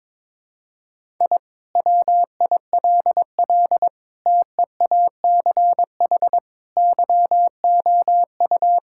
Morse code, heard as a keyed tone, about 22 wpm, 700 Hz. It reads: I WILL TEACH YOU
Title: Sentences from Top 1000 Words - Review Part 6 - 22wpm